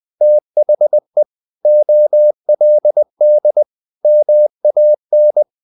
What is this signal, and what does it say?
Morse code, heard as a keyed tone, about 20 wpm, 600 Hz. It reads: THE OLD MAN
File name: the old man and the sea part 2 20wpm